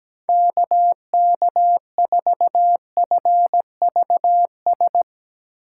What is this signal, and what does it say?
Morse code, heard as a keyed tone, about 17 wpm, 700 Hz. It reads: KK4FVS
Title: USA Call Signs - 6 Characters - 3x Character Spacing - 17wpm